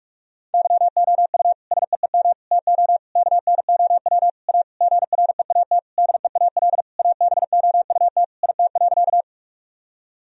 Morse code, heard as a keyed tone, about 46 wpm, 700 Hz. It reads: YOU SEEM TO KNOW A GREAT DEAL ABOUT IT.
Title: Sentences from Top 1000 Words - Review Part 1 - 45wpm